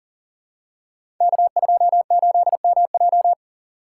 Morse code, heard as a keyed tone, about 42 wpm, 700 Hz. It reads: K28MJ
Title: Code Groups - 5 Letter:Numbers - 40wpm